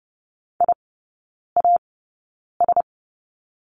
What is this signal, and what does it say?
Morse code, heard as a keyed tone, about 30 wpm, spaced as for 9 wpm, 700 Hz. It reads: IAS
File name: Lesson 17e - Letter u - Three Characters 30wpm